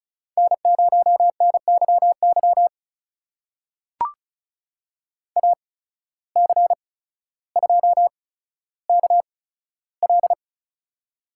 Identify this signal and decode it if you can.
Morse code, heard as a keyed tone, about 36 wpm, spaced as for 15 wpm, 700 Hz. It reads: N0NYY AC2KL
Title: USA Call Signs - 5 Characters - 8x Character Spacing - 35wpm